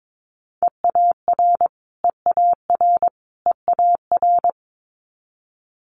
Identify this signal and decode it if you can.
Morse code, heard as a keyed tone, about 22 wpm, 700 Hz. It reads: EAR EAR EAR
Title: Multi-letter Phonemes Repeated 3x 22wpm